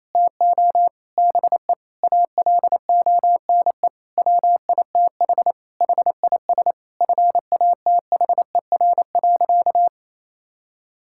Morse code, heard as a keyed tone, about 28 wpm, 700 Hz. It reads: TO BE ALONE WITH HIS FATHER.
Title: Sentences from Top 1000 Words - Review Part 5 - 28wpm